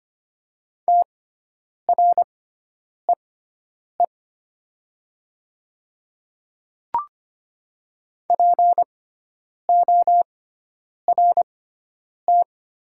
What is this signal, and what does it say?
Morse code, heard as a keyed tone, about 25 wpm, spaced as for 9 wpm, 700 Hz. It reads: TREE PORT
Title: 6x Character Spacing - Top 300 Words - 25wpm